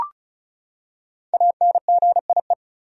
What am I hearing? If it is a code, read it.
Morse code, heard as a keyed tone, about 36 wpm, 700 Hz. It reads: ANGIE
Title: USA Common Names - 35wpm